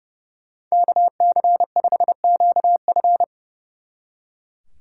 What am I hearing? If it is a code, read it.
Morse code, heard as a keyed tone, about 36 wpm, 700 Hz. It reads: KC5QF
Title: Speed Racing - USA 2x2 Call Signs 30wpm